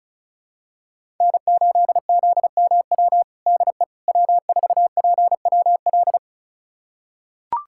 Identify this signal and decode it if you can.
Morse code, heard as a keyed tone, about 36 wpm, 700 Hz. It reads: N8ZMW DE W4PWL
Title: USA Callsign DE Callsign - 5 Characters - 35wpm